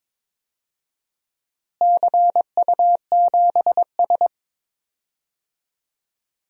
Morse code, heard as a keyed tone, about 22 wpm, 700 Hz. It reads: CU7S
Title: DXCC-100 Call Signs - 4 Characters - Repeated 2x - 22wpm